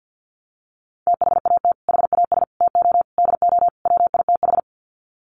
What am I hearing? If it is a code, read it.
Morse code, heard as a keyed tone, about 50 wpm, 700 Hz. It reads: THAT HAS TO DO WITH
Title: Sets of 5 Words Encore2 50wpm